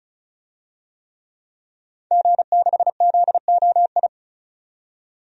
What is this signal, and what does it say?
Morse code, heard as a keyed tone, about 36 wpm, 700 Hz. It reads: G6ZOI
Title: DXCC-100 Call Signs - 5 Characters - 35wpm